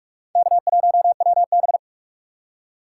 Morse code, heard as a keyed tone, about 42 wpm, 700 Hz. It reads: K1WB